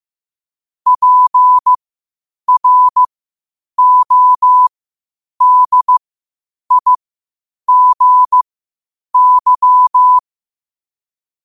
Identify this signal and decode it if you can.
Morse code, heard as a keyed tone, about 15 wpm, spaced as for 8 wpm, 1000 Hz. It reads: PRODIGY
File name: qotdfiles / 8wpm